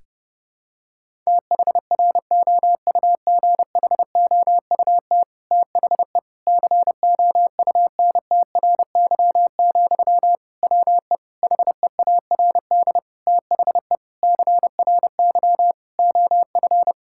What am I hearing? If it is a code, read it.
Morse code, heard as a keyed tone, about 32 wpm, 700 Hz. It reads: THROUGHOUT THE COUNTRY, WE HEARD THE CRY OF
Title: Sentences from Top 1000 Words - Review Part 5 - 30wpm